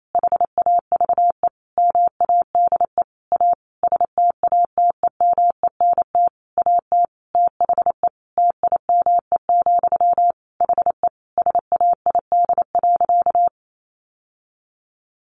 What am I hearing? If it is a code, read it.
Morse code, heard as a keyed tone, about 28 wpm, 700 Hz. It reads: HAVE MADE A STATEMENT AT THE TIME, HE SAID.
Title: Sentences from Top 1200 Words 28wpm